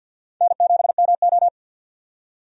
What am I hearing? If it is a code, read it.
Morse code, heard as a keyed tone, about 45 wpm, 700 Hz. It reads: N7MO
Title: USA Call Signs - 4 Characters - 50wpm